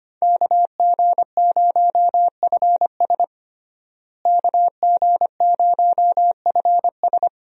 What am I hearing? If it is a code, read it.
Morse code, heard as a keyed tone, about 25 wpm, 700 Hz. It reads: KG0FS KG0FS
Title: USA Call Signs - 5 Characters - Repeated 2x - 25wpm